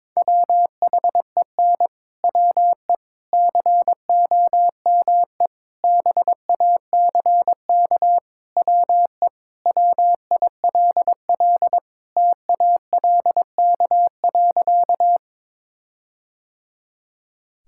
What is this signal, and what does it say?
Morse code, heard as a keyed tone, about 22 wpm, 700 Hz. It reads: WHEN WE COME BACK WE WILL TALK.
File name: practice / 22wpm / Sentences with 4-Word Phrases - 22wpm